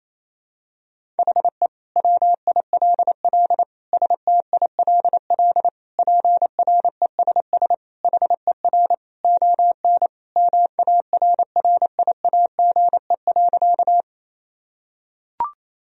Morse code, heard as a keyed tone, about 28 wpm, 700 Hz. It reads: HE WILL STILL PRESS HER ON MARRIAGE.
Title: Sentences from Top 1000 Words - Review Part 1 - 28wpm